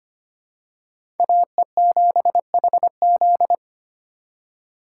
Morse code, heard as a keyed tone, about 25 wpm, 700 Hz. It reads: AE7HZ
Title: USA Call Signs - 5 Characters - 2x Character Spacing - 25wpm